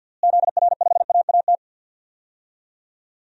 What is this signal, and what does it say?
Morse code, heard as a keyed tone, about 51 wpm, 700 Hz. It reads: ZR5AAT
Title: DXCC-100 Call Signs - 6 Characters - Repeated 2x - 50wpm